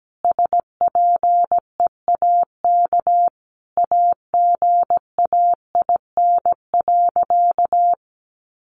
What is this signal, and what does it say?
Morse code, heard as a keyed tone, about 17 wpm, 700 Hz. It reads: SPEAK AGAIN.
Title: Sentences from Top 1500 Words 17wpm